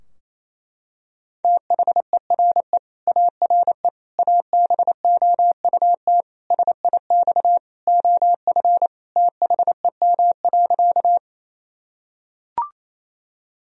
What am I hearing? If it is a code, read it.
Morse code, heard as a keyed tone, about 28 wpm, 700 Hz. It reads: THERE ARE ABOUT SIX OF THEM.
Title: Sentences from Top 1000 Words - Review Part 2 - 28wpm